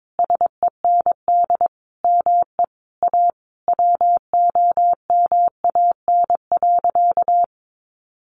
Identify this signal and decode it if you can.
Morse code, heard as a keyed tone, about 22 wpm, 700 Hz. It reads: SEND ME A WOMAN.
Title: Sentences from Top 1500 Words 22wpm